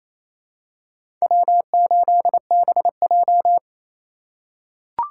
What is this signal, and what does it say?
Morse code, heard as a keyed tone, about 28 wpm, 700 Hz. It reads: W8BJ
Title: USA Call Signs - 4 Characters - 3x Character Spacing - 28wpm